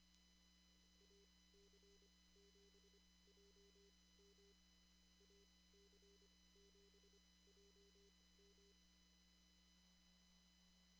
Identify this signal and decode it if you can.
Morse code, heard as a keyed tone, about 25 wpm, spaced as for 18 wpm, 400 Hz. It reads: AC7JM AC7JM
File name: CW / kt5j wpx cw-op-k5pi